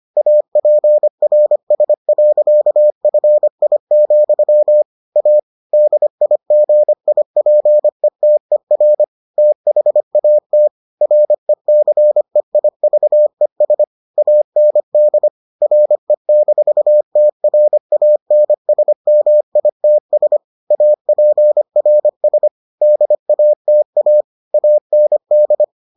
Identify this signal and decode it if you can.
Morse code, heard as a keyed tone, about 25 wpm, 600 Hz. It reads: APRS.FI, A DIGIPETER THAT RECEIVES AND RE-TRANSMITS APRS DATA AND